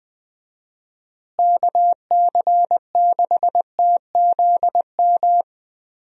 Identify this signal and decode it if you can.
Morse code, heard as a keyed tone, about 20 wpm, 700 Hz. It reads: KC6TZM
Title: USA Call Signs - 6 Characters - Repeated 2x - 20wpm